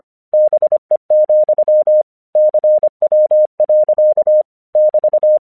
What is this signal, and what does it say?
Morse code, heard as a keyed tone, about 24 wpm, 610 Hz. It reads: BE, CW. <BT>
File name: 1 English 25wpm / AQC08